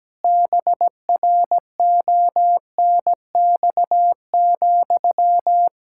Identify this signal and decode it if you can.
Morse code, heard as a keyed tone, about 17 wpm, 700 Hz. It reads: BRONX,
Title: Practice QSO - Exchange 1 - Beginning - 17wpm